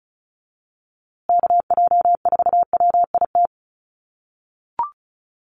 Morse code, heard as a keyed tone, about 34 wpm, 700 Hz. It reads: KJ4WIT
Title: USA Call Signs Repeated 3x 35wpm